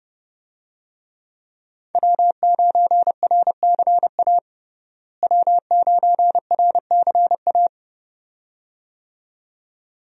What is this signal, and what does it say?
Morse code, heard as a keyed tone, about 30 wpm, 700 Hz. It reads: W9RCA W9RCA
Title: USA Call Signs - 5 Characters - Repeated 2x - 30wpm